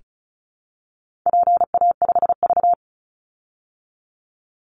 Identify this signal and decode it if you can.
Morse code, heard as a keyed tone, about 34 wpm, 700 Hz. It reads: PA5V
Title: Call Signs Repeated 3x - 35wpm